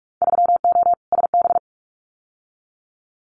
Morse code, heard as a keyed tone, about 46 wpm, 700 Hz. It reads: 3O SB
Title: Winter Field Day Exchanges Encore 45wpm